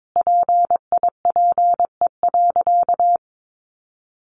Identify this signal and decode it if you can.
Morse code, heard as a keyed tone, about 22 wpm, 700 Hz. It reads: PIPE.
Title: Sentences from Top 1500 Words 22wpm